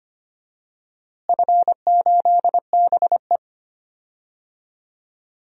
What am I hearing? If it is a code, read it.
Morse code, heard as a keyed tone, about 25 wpm, 700 Hz. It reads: F8BE
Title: DXCC-100 Call Signs - 4 Characters - Repeated 2x - 25wpm